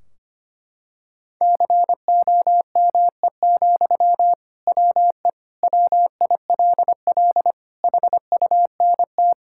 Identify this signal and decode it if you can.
Morse code, heard as a keyed tone, about 25 wpm, 700 Hz. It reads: COME, WE WILL HUNT
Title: Sentences from Top 1600 Words 25wpm